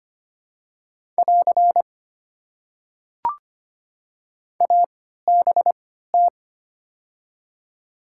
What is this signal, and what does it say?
Morse code, heard as a keyed tone, about 25 wpm, spaced as for 14 wpm, 690 Hz. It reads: <AR> ABT
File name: Top 90 QSO Elements - 3x Character Spacing - 25wpm